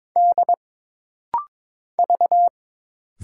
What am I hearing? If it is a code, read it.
Morse code, heard as a keyed tone, about 22 wpm, 700 Hz. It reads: D V
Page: This rapid-fire timing is geared to push your ICR (Instant Character Recognition) skills.